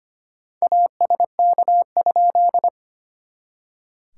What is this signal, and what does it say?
Morse code, heard as a keyed tone, about 25 wpm, 700 Hz. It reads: ASK?
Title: SOTA QSO Practice 25wpm